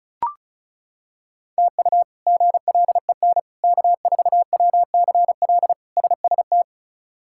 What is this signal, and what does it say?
Morse code, heard as a keyed tone, about 36 wpm, 700 Hz. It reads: TU GLEN K4WCL SST
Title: K1USN SST QSO Practice - 35wpm